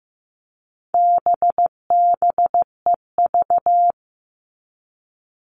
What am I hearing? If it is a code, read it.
Morse code, heard as a keyed tone, about 15 wpm, 700 Hz. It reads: BBEV
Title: All Cut Numbers - 4-Digit 15wpm